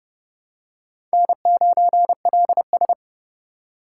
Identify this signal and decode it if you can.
Morse code, heard as a keyed tone, about 30 wpm, 700 Hz. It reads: N9LS
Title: USA Call Signs - 8x Character Spacing - 30wpm